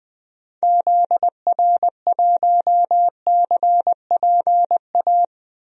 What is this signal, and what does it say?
Morse code, heard as a keyed tone, about 20 wpm, 700 Hz. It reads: ZR1CPA
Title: DXCC-100 Call Signs - 6 Characters - 20wpm